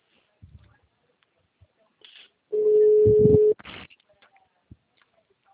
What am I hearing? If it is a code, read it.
Ringback tone (Europe, 425 Hz).